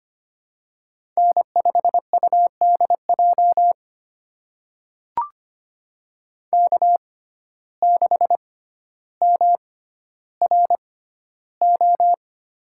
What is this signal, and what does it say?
Morse code, heard as a keyed tone, about 25 wpm, spaced as for 13 wpm, 700 Hz. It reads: N5UDJ K6MRO
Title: USA Call Signs - 6x Character Spacing - 25wpm